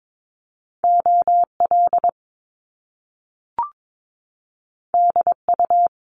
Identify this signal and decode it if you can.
Morse code, heard as a keyed tone, about 22 wpm, 700 Hz. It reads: OL DU
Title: Top Two Letter Combinations in Words 22wpm